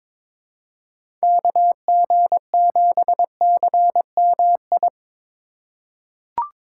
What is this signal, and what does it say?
Morse code, heard as a keyed tone, about 22 wpm, 700 Hz. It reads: KG7CMI
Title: USA Call Signs - 6 Characters - 4x Character Spacing - 22wpm